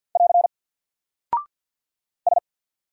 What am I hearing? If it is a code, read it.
Morse code, heard as a keyed tone, about 48 wpm, 700 Hz. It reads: <AR> S
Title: Single Character (Letter:Number:Punctuation:Prosign) Alt 50wpm